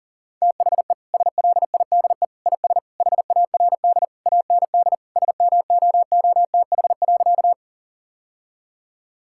Morse code, heard as a keyed tone, about 37 wpm, 700 Hz. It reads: THE SLIDE IS HARD AND SMOOTH.
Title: Sentences from Top 1600 Words 40wpm